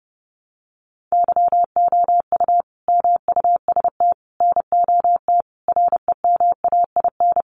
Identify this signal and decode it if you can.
Morse code, heard as a keyed tone, about 30 wpm, 700 Hz. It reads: YOU MUST NOT REMAIN
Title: Sentences from Top 1200 Words 30wpm